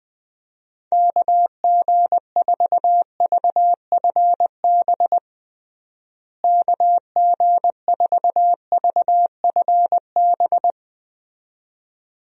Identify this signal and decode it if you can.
Morse code, heard as a keyed tone, about 20 wpm, 700 Hz. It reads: KG4VFB KG4VFB
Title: USA Call Signs - 6 Characters - Repeated 2x - 20wpm